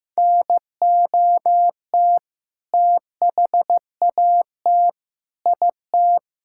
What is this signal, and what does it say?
Morse code, heard as a keyed tone, about 16 wpm, 700 Hz. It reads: NOT THAT IT
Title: Sentences from Top 1000 Words - Review Part 5 - 15wpm